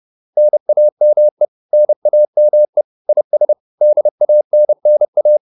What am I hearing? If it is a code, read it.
Morse code, heard as a keyed tone, about 30 wpm, 600 Hz. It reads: NAME NAME IS DANNA
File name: QSO406 30